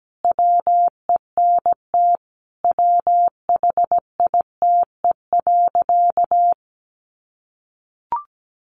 Morse code, heard as a keyed tone, about 17 wpm, 700 Hz. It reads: WENT WHITE.
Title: Sentences from Top 1200 Words 17wpm